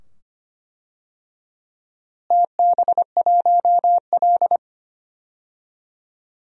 Morse code, heard as a keyed tone, about 25 wpm, 700 Hz. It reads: TB1L
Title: DXCC-100 Call Signs - 4 Characters - Repeated 2x - 25wpm